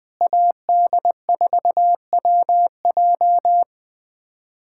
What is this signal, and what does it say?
Morse code, heard as a keyed tone, about 20 wpm, 700 Hz. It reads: AD4WJ
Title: USA Callsign DE Callsign 20wpm